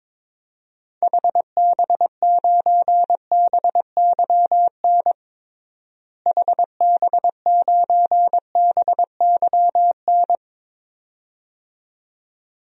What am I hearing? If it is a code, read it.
Morse code, heard as a keyed tone, about 22 wpm, 700 Hz. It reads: HB9BYN HB9BYN
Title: DXCC-100 Call Signs - 6 Characters - Repeated 2x - 22wpm